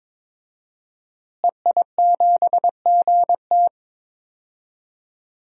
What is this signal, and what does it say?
Morse code, heard as a keyed tone, about 22 wpm, 700 Hz. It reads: EI7GT
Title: DXCC-100 Call Signs - 5 Characters - Repeated 3x - 22wpm